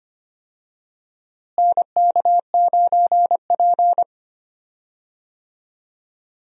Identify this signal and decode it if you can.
Morse code, heard as a keyed tone, about 25 wpm, 700 Hz. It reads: NK9P
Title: USA Call Signs - 4 Characters - Repeated 2x - 25wpm